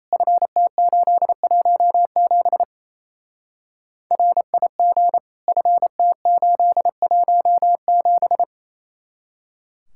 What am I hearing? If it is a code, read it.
Morse code, heard as a keyed tone, about 33 wpm, 700 Hz. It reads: FT817 RIG FT817
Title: Speed-Racing Common QSO Phrases - 28wpm